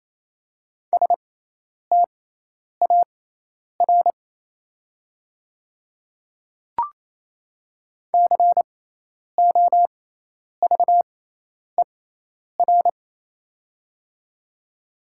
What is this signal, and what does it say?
Morse code, heard as a keyed tone, about 28 wpm, spaced as for 10 wpm, 700 Hz. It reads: STAR COVER
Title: 6x Character Spacing - Top 500 Words - 28wpm